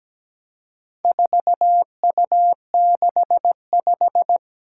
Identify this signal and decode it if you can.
Morse code, heard as a keyed tone, about 17 wpm, 700 Hz. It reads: 4U65
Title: Four Challenging Letter-Number Combinations 17wpm